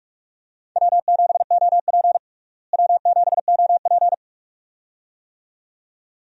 Morse code, heard as a keyed tone, about 41 wpm, 700 Hz. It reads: W7OP W7OP
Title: USA Call Signs - 4 Characters - Repeated 2x - 45wpm